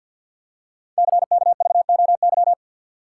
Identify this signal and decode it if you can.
Morse code, heard as a keyed tone, about 52 wpm, 700 Hz. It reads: CKVOY
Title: Code Groups - 5 Letters - 50wpm